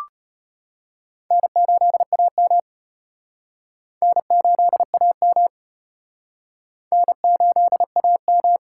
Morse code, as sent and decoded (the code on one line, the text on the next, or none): -. ---.. .- -- / -. ---.. .- -- / -. ---.. .- --
N8AM N8AM N8AM